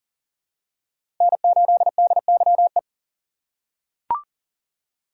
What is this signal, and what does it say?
Morse code, heard as a keyed tone, about 40 wpm, 700 Hz. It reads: N8DYE
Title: USA Call Signs - 5 Characters - 2x Character Spacing - 40wpm